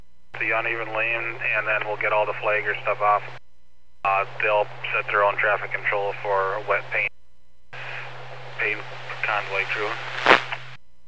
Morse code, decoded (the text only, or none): none